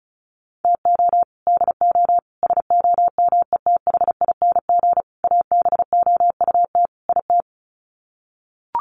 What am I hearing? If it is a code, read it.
Morse code, heard as a keyed tone, about 34 wpm, 700 Hz. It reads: TO DO SOMETHING ABOUT IT
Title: Sets of 5 Words 35wpm